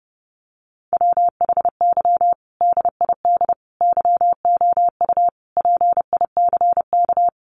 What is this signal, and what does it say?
Morse code, heard as a keyed tone, about 30 wpm, 700 Hz. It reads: WHY DID YOU PICK
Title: Sentences from Top 1000 Words 30wpm